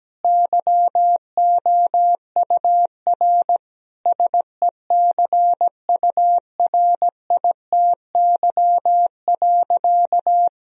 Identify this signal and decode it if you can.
Morse code, heard as a keyed tone, about 17 wpm, 700 Hz. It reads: YOUR SECURITY.
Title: Sentences from Top 1000 Words - Review Part 4 - 17wpm